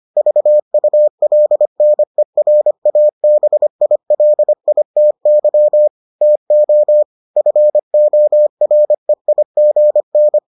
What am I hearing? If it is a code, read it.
Morse code, heard as a keyed tone, about 25 wpm, 600 Hz. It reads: VULNERABILITY TO FOREIGN